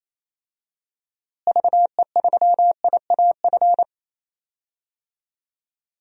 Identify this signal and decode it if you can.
Morse code, heard as a keyed tone, about 28 wpm, 700 Hz. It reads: VE3IAF
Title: DXCC-100 Call Signs - 6 Characters - Repeated 2x - 28wpm